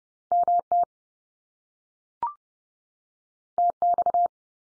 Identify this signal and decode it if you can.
Morse code, heard as a keyed tone, about 30 wpm, 700 Hz. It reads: MT TX
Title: US States 30WPM